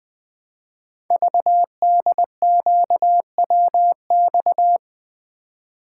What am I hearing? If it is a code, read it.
Morse code, heard as a keyed tone, about 21 wpm, 700 Hz. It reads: VDQWX
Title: Code Groups - 5 Letter:Numbers - 20wpm